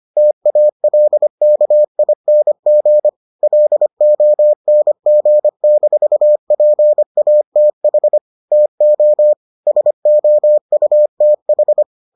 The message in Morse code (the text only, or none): TALKING LONG-PATH TO SOUTH